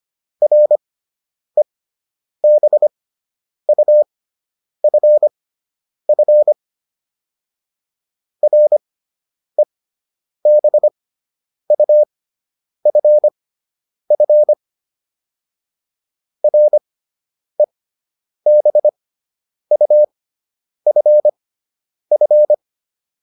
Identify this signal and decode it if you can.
Morse code, heard as a keyed tone, about 25 wpm, spaced as for 9 wpm, 600 Hz. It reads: REBUFF REBUFF REBUFF